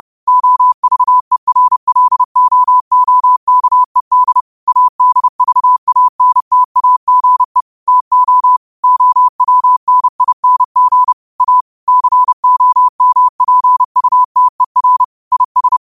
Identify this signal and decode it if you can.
Morse code, heard as a keyed tone, about 30 wpm, 1000 Hz. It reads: OVERLOOKED ADVANTAGE TO OWNING A COMPUTER IS
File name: qotdfiles / 30wpm